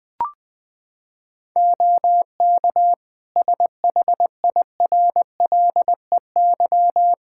Morse code, heard as a keyed tone, about 20 wpm, 700 Hz. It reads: OK SHIRLEY
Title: Practice QSO - Exchange 3 - More Info - 20wpm